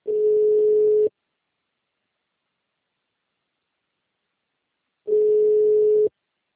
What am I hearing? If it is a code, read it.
Ringback tone (Europe, 425 Hz).
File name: out- -2013- - - .1323